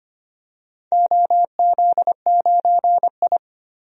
Morse code, heard as a keyed tone, about 25 wpm, 700 Hz. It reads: OZ9I
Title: DXCC-100 Call Signs - 4 Characters - Repeated 3x - 25wpm